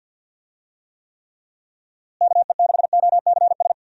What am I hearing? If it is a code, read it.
Morse code, heard as a keyed tone, about 52 wpm, 700 Hz. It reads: KE6OCS
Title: USA Call Signs - 6 Characters - Repeated 3x - 50wpm